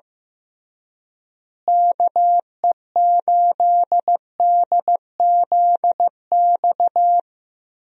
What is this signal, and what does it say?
Morse code, heard as a keyed tone, about 15 wpm, 700 Hz. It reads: KE8DZX
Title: USA Call Signs - 6 Characters - Repeated 3x - 15wpm